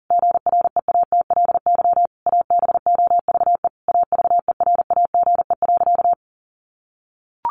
Morse code, heard as a keyed tone, about 40 wpm, 700 Hz. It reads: GREATLY ABOVE AVERAGE.
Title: Sentences from Top 1400 Words 40wpm